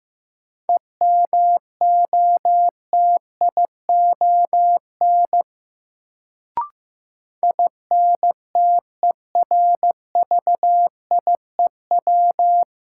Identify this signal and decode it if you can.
Morse code, heard as a keyed tone, about 15 wpm, 700 Hz. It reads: EMOTION INTERVIEW
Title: Sentences from Top 1600 Words - New Word Drill Alt - 15wpm